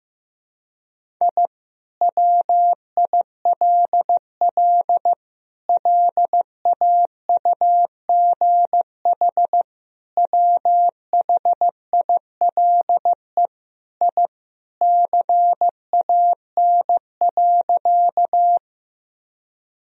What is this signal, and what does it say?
Morse code, heard as a keyed tone, about 16 wpm, 700 Hz. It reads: I WILL LAUGH WHILE I CAN.
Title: Sentences from Top 1000 Words - Review Part 1 - 15wpm